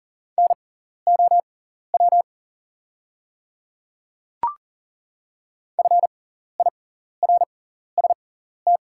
Morse code, heard as a keyed tone, about 40 wpm, spaced as for 14 wpm, 690 Hz. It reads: NOW FIRST
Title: 6x Character Spacing - Top 200 Words - 40wpm